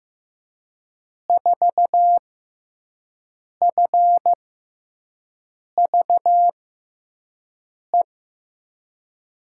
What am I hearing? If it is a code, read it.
Morse code, heard as a keyed tone, about 16 wpm, spaced as for 5 wpm, 700 Hz. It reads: 4FVE
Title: USA Call Signs - 6 Characters - 6x Character Spacing - 15wpm